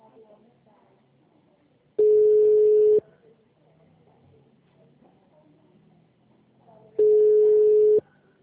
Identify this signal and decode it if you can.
Ringback tone (Europe, 425 Hz), with speech.